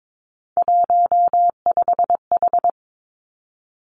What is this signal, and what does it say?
Morse code, heard as a keyed tone, about 22 wpm, 700 Hz. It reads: 15H
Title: Three Challenging Letter-Number Combinations 22wpm